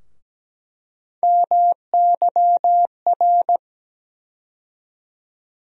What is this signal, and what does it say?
Morse code, heard as a keyed tone, about 17 wpm, 700 Hz. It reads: MYR
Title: USA Call Sign Postfixes - 3 Letters 17wpm